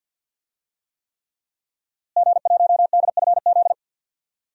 Morse code, heard as a keyed tone, about 52 wpm, 700 Hz. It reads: G1DFZ
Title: DXCC-100 Call Signs - 5 Characters - Repeated 3x - 50wpm